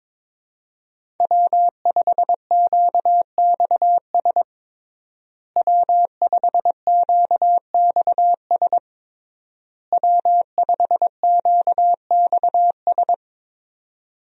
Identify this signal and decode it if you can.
Morse code, heard as a keyed tone, about 22 wpm, 700 Hz. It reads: W5QXS W5QXS W5QXS
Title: USA Call Signs - 5 Characters - Repeated 3x - 22wpm